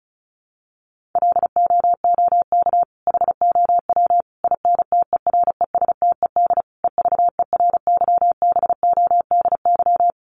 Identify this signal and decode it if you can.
Morse code, heard as a keyed tone, about 34 wpm, 700 Hz. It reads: LOOK HOW INTERESTED EVERYBODY
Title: Sentences from Top 1400 Words 35wpm